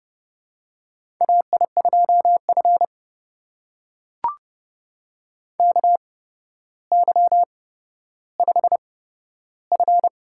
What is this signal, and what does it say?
Morse code, heard as a keyed tone, about 29 wpm, spaced as for 13 wpm, 700 Hz. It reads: AI2F KY5F
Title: USA Call Signs - 4 Characters - 8x Character Spacing - 30wpm